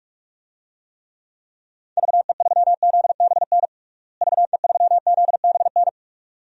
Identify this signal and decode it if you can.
Morse code, heard as a keyed tone, about 46 wpm, 700 Hz. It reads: VE3ZBN VE3ZBN
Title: DXCC-100 Call Signs - 6 Characters - Repeated 2x - 45wpm